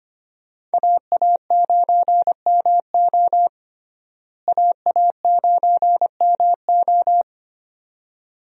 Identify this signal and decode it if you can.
Morse code, heard as a keyed tone, about 25 wpm, 700 Hz. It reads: AA9MO AA9MO
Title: USA Call Signs - 5 Characters - Repeated 2x - 25wpm